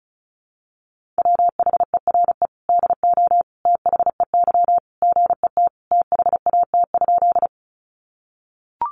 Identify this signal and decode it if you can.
Morse code, heard as a keyed tone, about 34 wpm, 700 Hz. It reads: WHERE DO THEY GET THAT?
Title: Sentences from Top 200 Words 35wpm